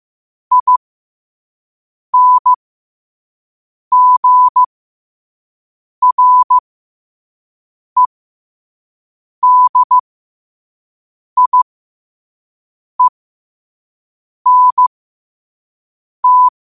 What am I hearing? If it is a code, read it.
Morse code, heard as a keyed tone, about 15 wpm, spaced as for 5 wpm, 1000 Hz. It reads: INGREDIENT
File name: qotdfiles / 5wpm